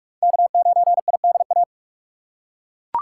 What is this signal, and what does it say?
Morse code, heard as a keyed tone, about 47 wpm, 700 Hz. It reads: K9IDA
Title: USA Call Signs - 5 Characters - 6x Character Spacing - 45wpm